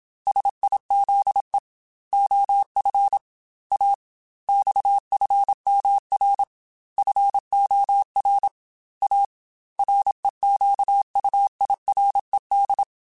Morse code, heard as a keyed tone, about 27 wpm, 800 Hz. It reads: SIZE OF A XFMR FOR A REQUIRED
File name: Technical Subjects / Technical Subjects-mp3s / Sw Pwr Sup @25